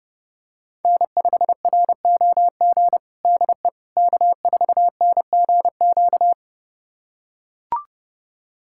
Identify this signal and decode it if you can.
Morse code, heard as a keyed tone, about 30 wpm, 700 Hz. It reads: N5ROG DE K4NGQ
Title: USA Callsign DE Callsign - 5 Characters - 30wpm